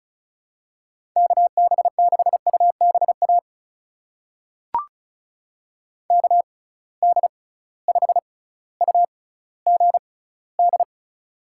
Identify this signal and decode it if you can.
Morse code, heard as a keyed tone, about 36 wpm, spaced as for 18 wpm, 700 Hz. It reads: KB6UBA KD5UGD
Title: USA Call Signs - 6x Character Spacing - 35wpm